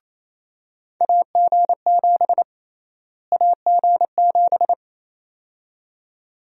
Morse code, heard as a keyed tone, about 28 wpm, 700 Hz. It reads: AG7 AG7
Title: USA Call Sign Prefixes - Repeated 2x - 28wpm